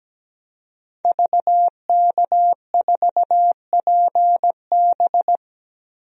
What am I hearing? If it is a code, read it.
Morse code, heard as a keyed tone, about 17 wpm, 700 Hz. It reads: VK4PB
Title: DXCC-100 Call Signs - 5 Characters - 17wpm